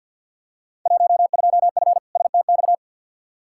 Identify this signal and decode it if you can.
Morse code, heard as a keyed tone, about 47 wpm, 700 Hz. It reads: 12F STX
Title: Field Day Exchanges Encore - 50wpm